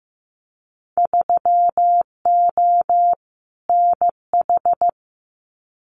Morse code, heard as a keyed tone, about 15 wpm, 700 Hz. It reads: 3O NH
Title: Winter Field Day Exchanges Encore 15wpm